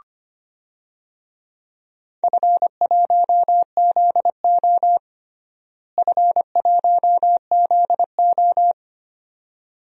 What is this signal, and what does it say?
Morse code, heard as a keyed tone, about 25 wpm, 700 Hz. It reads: F1ZO F1ZO
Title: DXCC-100 Call Signs - 4 Characters - Repeated 2x - 25wpm